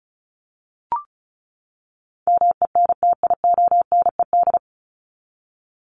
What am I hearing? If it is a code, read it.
Morse code, heard as a keyed tone, about 34 wpm, 700 Hz. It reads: MENTIONED
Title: ED Suffix - Top 100 Words with Base Words 35wpm